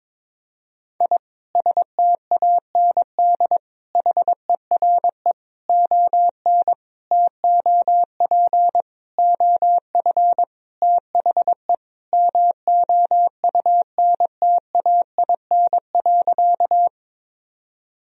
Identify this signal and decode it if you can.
Morse code, heard as a keyed tone, about 22 wpm, 700 Hz. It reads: I STAND HERE ON TOP OF THE MOUNTAIN.
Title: Sentences from Top 1000 Words - Review Part 1 - 22wpm